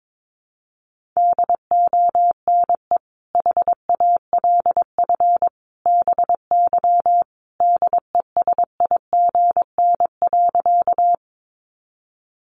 Morse code, heard as a keyed tone, about 22 wpm, 700 Hz. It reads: DONE HALF BY DESIGN.
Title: Sentences from Top 800 Words 22wpm